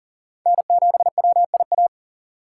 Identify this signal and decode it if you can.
Morse code, heard as a keyed tone, about 40 wpm, 700 Hz. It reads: N7WIA
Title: USA Call Signs - 3x Character Spacing - 40wpm